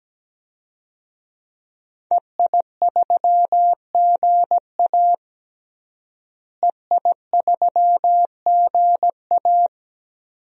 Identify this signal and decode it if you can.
Morse code, heard as a keyed tone, about 17 wpm, 700 Hz. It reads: EI3GA EI3GA
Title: DXCC-100 Call Signs - 5 Characters - Repeated 2x - 17wpm